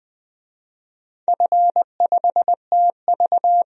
Morse code, heard as a keyed tone, about 20 wpm, 700 Hz. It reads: F5TV
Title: DXCC-100 Call Signs - 4 Characters - 20wpm